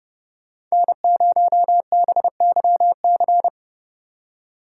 Morse code, heard as a keyed tone, about 35 wpm, 700 Hz. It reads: N0BYC
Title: Speed Racing - USA 1x3 Call Signs 30wpm